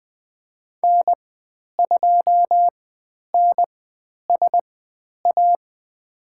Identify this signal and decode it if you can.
Morse code, heard as a keyed tone, about 20 wpm, spaced as for 10 wpm, 700 Hz. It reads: N2NSA
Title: 095 Lesson 16g - Letter c - Call Signs 20wpm